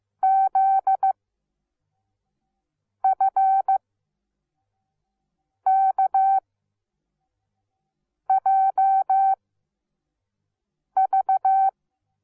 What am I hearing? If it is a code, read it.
Morse code, heard as a keyed tone, about 15 wpm, spaced as for 4 wpm, 770 Hz. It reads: ZFKJV
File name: Onlinelehrgang / morsen / DJ4UF Morsen17